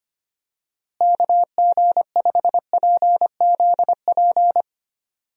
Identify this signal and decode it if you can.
Morse code, heard as a keyed tone, about 25 wpm, 700 Hz. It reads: KG5PZP
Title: USA Call Signs - 6 Characters - Repeated 3x - 25wpm